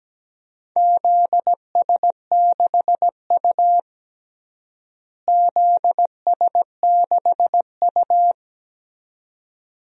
Morse code, heard as a keyed tone, about 17 wpm, 700 Hz. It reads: ZS6U ZS6U
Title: DXCC-100 Call Signs - 4 Characters - Repeated 2x - 17wpm